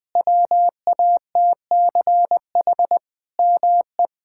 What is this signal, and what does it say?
Morse code, heard as a keyed tone, about 20 wpm, 700 Hz. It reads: WATCH ME
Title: Sentences from Top 1600 Words 20wpm